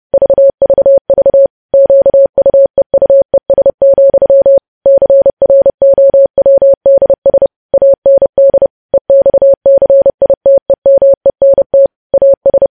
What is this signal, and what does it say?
Morse code, heard as a keyed tone, about 30 wpm, 550 Hz. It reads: VVV QUEUES, CROWDS AND EXCITEMENT AS
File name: podcast / News headlines-1x-30x30